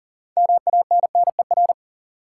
Morse code, heard as a keyed tone, about 37 wpm, 700 Hz. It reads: MANNER
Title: Six Letter Words - 40wpm